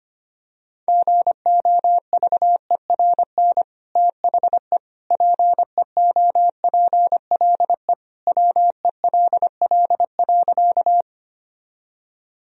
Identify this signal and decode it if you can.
Morse code, heard as a keyed tone, about 25 wpm, 700 Hz. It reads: GOVERN THE PEOPLE WELL.
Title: Sentences from Top 1000 Words - Review Part 2 - 25wpm